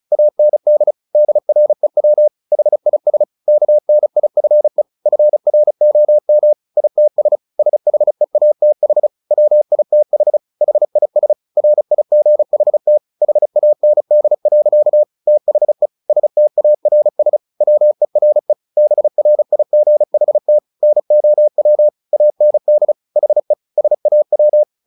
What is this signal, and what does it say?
Morse code, heard as a keyed tone, about 35 wpm, 600 Hz. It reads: AND DREW HIS KNIFE FROM ITS SHEATH WITH HIS RIGHT HAND. THE STARS WERE BRIGHT NOW AND HE SAW
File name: the old man and the sea 2 35wpm